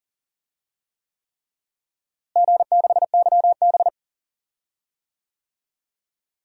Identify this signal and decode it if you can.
Morse code, heard as a keyed tone, about 40 wpm, 700 Hz. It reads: G6YB
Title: DXCC-100 Call Signs - 4 Characters - Repeated 3x - 40wpm